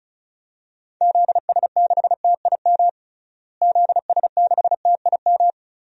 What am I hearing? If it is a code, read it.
Morse code, heard as a keyed tone, about 36 wpm, 700 Hz. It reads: ZS6TIM ZS6TIM
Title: DXCC-100 Call Signs - 6 Characters - Repeated 2x - 35wpm